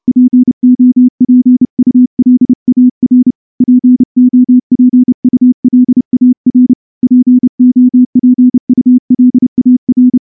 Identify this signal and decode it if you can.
Morse code, heard as a keyed tone, about 28 wpm, 250 Hz. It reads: POPULAR POPULAR POPULAR
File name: morse-training-30WPM